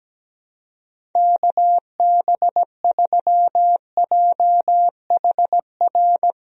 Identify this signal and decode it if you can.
Morse code, heard as a keyed tone, about 17 wpm, 700 Hz. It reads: KB3JHR
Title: USA Call Signs - 6 Characters - Repeated 2x - 17wpm